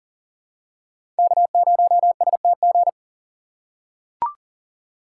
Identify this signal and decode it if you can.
Morse code, heard as a keyed tone, about 40 wpm, 700 Hz. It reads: K0STG
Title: USA Call Signs - 5 Characters - 2x Character Spacing - 40wpm